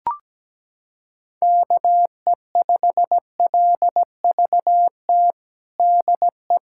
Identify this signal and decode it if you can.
Morse code, heard as a keyed tone, about 17 wpm, 700 Hz. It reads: KE5LVT DE
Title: USA Callsign DE Callsign - 6 Characters - 17wpm